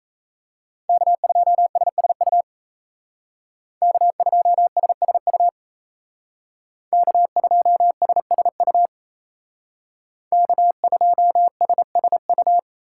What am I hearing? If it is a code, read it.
Morse code, heard as a keyed tone, about 37 wpm, 700 Hz. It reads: K2SSU K2SSU K2SSU K2SSU